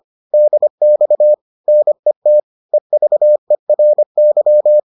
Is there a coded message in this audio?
Yes: Morse code from 0.3 to 4.8 s, heard as a keyed tone, about 25 wpm, 600 Hz. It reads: DX NET EVERY